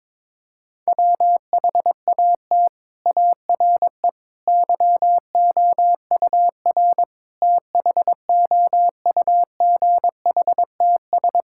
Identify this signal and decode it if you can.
Morse code, heard as a keyed tone, about 22 wpm, 700 Hz. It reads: WHAT ARE YOUR THOUGHTS
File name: practice / 22wpm / Sets of 6 Words - 22wpm